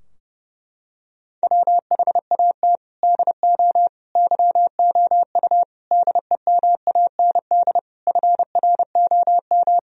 Morse code, heard as a keyed tone, about 30 wpm, 700 Hz. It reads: WHAT DO YOU DEMAND FROM
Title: Sentences from Top 1000 Words - Review Part 4 - 30wpm